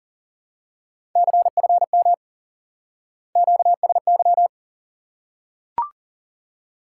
Morse code, heard as a keyed tone, about 40 wpm, 700 Hz. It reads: CFM QSY
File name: IQE Course 3 - Lesson 89 - YAGI - Two Elements - 40wpm